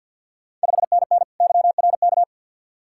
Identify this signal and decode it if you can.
Morse code, heard as a keyed tone, about 52 wpm, 700 Hz. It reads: 5NN YRK
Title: Ontario QSO Party Exchange 50wpm